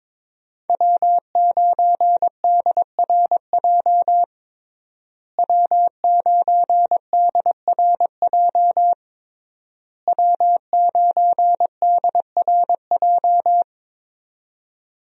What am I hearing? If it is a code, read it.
Morse code, heard as a keyed tone, about 22 wpm, 700 Hz. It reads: W9DRJ W9DRJ W9DRJ
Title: USA Call Signs - 5 Characters - Repeated 3x - 22wpm